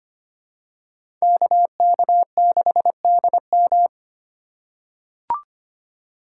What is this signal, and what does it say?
Morse code, heard as a keyed tone, about 25 wpm, 700 Hz. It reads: KK6DM
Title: USA Call Signs - 5 Characters - 2x Character Spacing - 25wpm